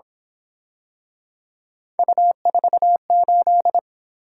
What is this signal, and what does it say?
Morse code, heard as a keyed tone, about 26 wpm, 700 Hz. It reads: U48
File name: Speed-Racing - Three Letter-Number Combinations - 22wpm